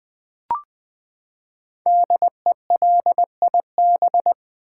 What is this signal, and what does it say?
Morse code, heard as a keyed tone, about 20 wpm, 700 Hz. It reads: DELIB
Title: Build A Word - 5 Letters At A Time - 12-Letter Words - 20wpm